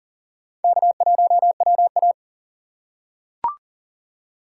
Morse code, heard as a keyed tone, about 40 wpm, 700 Hz. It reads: K1WA
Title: USA Call Signs - 4 Characters - 4x Character Spacing - 40wpm